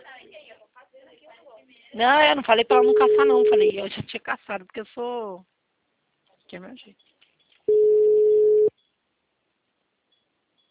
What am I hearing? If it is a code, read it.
Ringback tone (Europe, 425 Hz), with speech.
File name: out- -2046- - - .6153